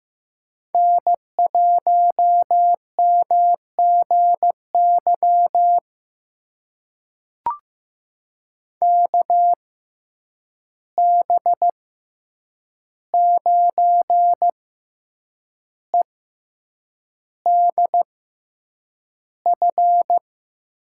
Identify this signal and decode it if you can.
Morse code, heard as a keyed tone, about 15 wpm, spaced as for 7 wpm, 700 Hz. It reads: N1MGY KB9EDF
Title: USA Call Signs - 6x Character Spacing - 15wpm